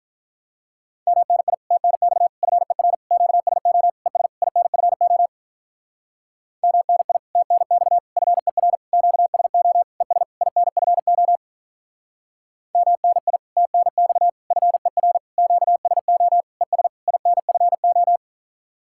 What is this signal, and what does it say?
Morse code, heard as a keyed tone, about 45 wpm, 700 Hz. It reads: MNI TNX FER QSO ES INFO MNI TNX FER QSO ES INFO MNI TNX FER QSO ES INFO